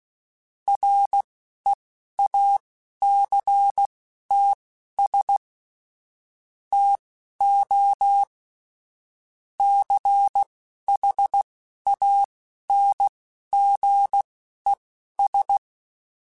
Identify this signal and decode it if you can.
Morse code, heard as a keyed tone, about 16 wpm, spaced as for 11 wpm, 800 Hz. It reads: REACTS TO CHANGES
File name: Technical Subjects / Technical Subjects-mp3s / Reactance @10